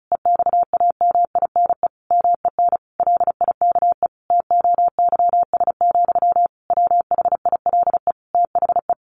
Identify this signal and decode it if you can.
Morse code, heard as a keyed tone, about 34 wpm, 700 Hz. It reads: EXAMINE MEN LIKE TOYS, WHILE THE
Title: A Sound Of Thunder 35wpm